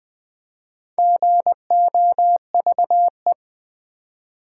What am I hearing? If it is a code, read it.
Morse code, heard as a keyed tone, about 20 wpm, 700 Hz. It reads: GOVE
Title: Build A Word - 2 Letters At A Time - 6-Letter Words - 20wpm